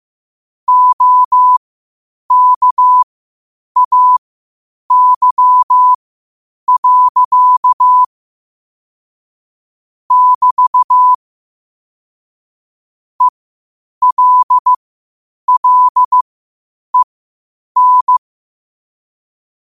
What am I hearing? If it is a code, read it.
Morse code, heard as a keyed tone, about 15 wpm, spaced as for 8 wpm, 1000 Hz. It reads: OKAY. <BT> ELLEN
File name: qotdfiles / 8wpm